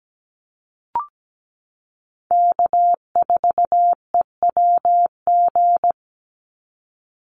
Morse code, heard as a keyed tone, about 17 wpm, 700 Hz. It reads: K4EWG
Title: Call Signs Repeated 3x - 17wpm